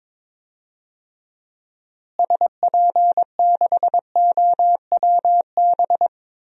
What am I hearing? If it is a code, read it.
Morse code, heard as a keyed tone, about 21 wpm, 700 Hz. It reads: SP6OWB
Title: DXCC-100 Call Signs - 6 Characters - Repeated 2x - 22wpm